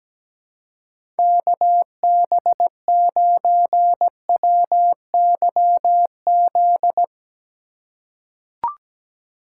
Morse code, heard as a keyed tone, about 17 wpm, 700 Hz. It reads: KB9WYZ
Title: USA Call Signs - 6 Characters - 2x Character Spacing - 17wpm